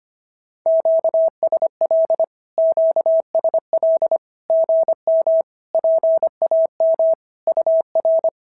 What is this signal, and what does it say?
Morse code, heard as a keyed tone, about 25 wpm, 650 Hz. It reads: QSL QSL GM PAM UR
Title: SOTA QSO Practice 25wpm